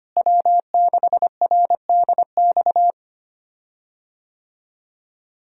Morse code, heard as a keyed tone, about 25 wpm, 700 Hz. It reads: W6RDX
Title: USA Call Signs - 5 Characters - Repeated 2x - 25wpm